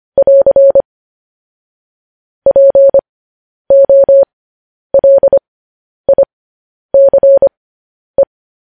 Morse code, heard as a keyed tone, about 25 wpm, spaced as for 10 wpm, 550 Hz. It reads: <AR> POLICE
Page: news headlines in morse code at 10 wpm